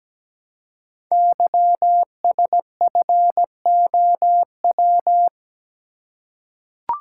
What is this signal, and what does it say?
Morse code, heard as a keyed tone, about 17 wpm, 700 Hz. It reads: YSFOW
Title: Code Groups - 5 Letter:Numbers - 17wpm